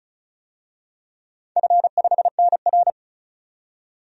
Morse code, heard as a keyed tone, about 36 wpm, 690 Hz. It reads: F5NR
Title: DXCC-100 Call Signs - 4 Characters - 35wpm